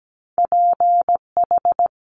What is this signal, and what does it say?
Morse code, heard as a keyed tone, about 17 wpm, 700 Hz. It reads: PH
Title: Top Two Letter Combinations in Words 17wpm